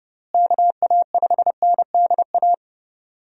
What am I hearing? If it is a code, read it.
Morse code, heard as a keyed tone, about 31 wpm, 700 Hz. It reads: KA5NDA
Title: USA Call Signs - 6 Characters - Repeated 3x - 30wpm